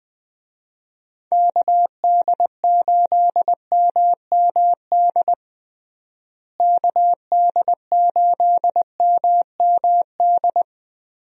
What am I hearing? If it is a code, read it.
Morse code, heard as a keyed tone, about 20 wpm, 700 Hz. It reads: KD8MMD KD8MMD
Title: USA Call Signs - 6 Characters - Repeated 3x - 20wpm